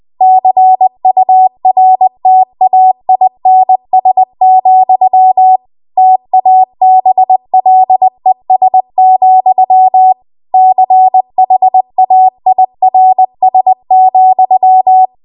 Morse code, heard as a keyed tone, about 20 wpm, 750 Hz. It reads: CURTAINS, TABLES, CHAIRS,